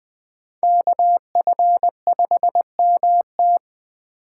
Morse code, heard as a keyed tone, about 20 wpm, 700 Hz. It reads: KF5MT
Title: USA Call Signs - 5 Characters - 8x Character Spacing - 20wpm